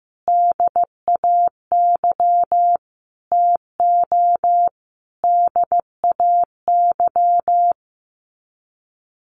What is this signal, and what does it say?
Morse code, heard as a keyed tone, about 15 wpm, 700 Hz. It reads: DAY TO DAY
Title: Sets of 5 Words Encore 15wpm